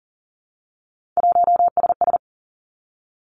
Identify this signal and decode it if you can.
Morse code, heard as a keyed tone, about 38 wpm, 700 Hz. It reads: 1SS
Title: Three Challenging Letter-Number Combinations 40wpm